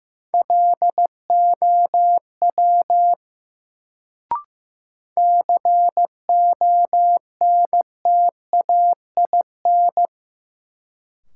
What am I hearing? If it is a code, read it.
Morse code, heard as a keyed tone, about 15 wpm, 700 Hz. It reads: LOW CONTAIN
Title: Sentences from Top 500 Words - New Word Drill - 15wpm